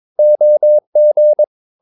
Morse code, heard as a keyed tone, about 22 wpm, 600 Hz. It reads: OG
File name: Ferie QRS 22wpm